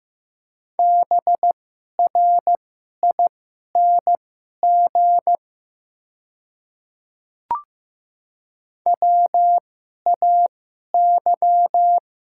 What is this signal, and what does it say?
Morse code, heard as a keyed tone, about 15 wpm, spaced as for 11 wpm, 700 Hz. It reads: BRING WAY
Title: 2x Character Spacing - Top 500 Words - 15wpm